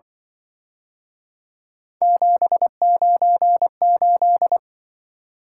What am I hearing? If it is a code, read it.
Morse code, heard as a keyed tone, about 25 wpm, 700 Hz. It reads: 798
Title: Speed-Racing - Three Letter-Number Combinations - 20wpm